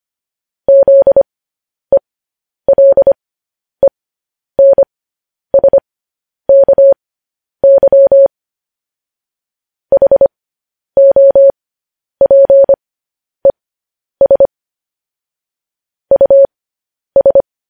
Morse code, heard as a keyed tone, about 25 wpm, spaced as for 10 wpm, 550 Hz. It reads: ZELENSKY HOPES US
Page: news-headlines-in-morse-code-at-10-wpm